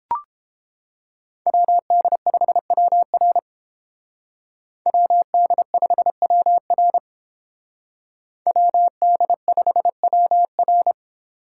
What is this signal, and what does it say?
Morse code, heard as a keyed tone, about 30 wpm, 700 Hz. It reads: WD5WR WD5WR WD5WR